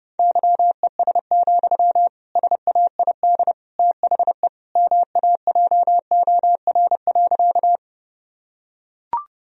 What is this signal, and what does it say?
Morse code, heard as a keyed tone, about 30 wpm, 700 Hz. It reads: YES, SAID THE MAJOR.
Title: Sentences from Top 1000 Words - Review Part 4 - 30wpm